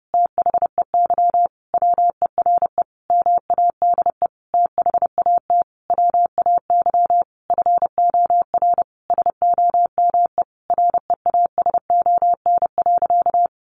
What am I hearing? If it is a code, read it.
Morse code, heard as a keyed tone, about 30 wpm, 700 Hz. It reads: THEY WERE MADE THAT WAY FOR SOME REASON.
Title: Sentences from Top 700 Words 30wpm